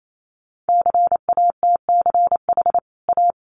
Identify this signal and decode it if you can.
Morse code, heard as a keyed tone, about 28 wpm, 700 Hz. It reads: CATCH A
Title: Sentences from Top 1500 Words 28wpm